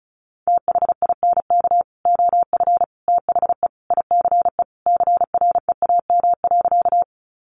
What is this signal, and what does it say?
Morse code, heard as a keyed tone, about 34 wpm, 700 Hz. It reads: THINK OF THE ICE CREAM.
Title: Sentences from Top 1500 Words 35wpm